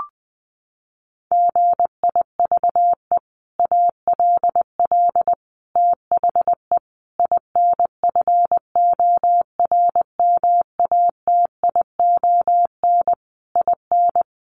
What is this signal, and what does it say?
Morse code, heard as a keyed tone, about 20 wpm, 700 Hz. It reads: GIVE ALL THE INFORMATION IN
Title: Sentences from Top 600 Words 20wpm